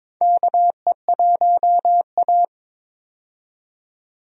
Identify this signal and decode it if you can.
Morse code, heard as a keyed tone, about 22 wpm, 700 Hz. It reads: KE1A
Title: USA Call Signs - 4 Characters - 22wpm